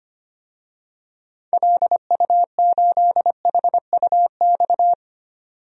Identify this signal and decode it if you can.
Morse code, heard as a keyed tone, about 25 wpm, 700 Hz. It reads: LU8HUX
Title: DXCC-100 Call Signs - 6 Characters - 25wpm